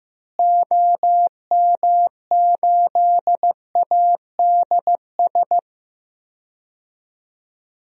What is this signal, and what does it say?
Morse code, heard as a keyed tone, about 15 wpm, 700 Hz. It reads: OM8ADS
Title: DXCC-100 Call Signs - 6 Characters - Repeated 3x - 15wpm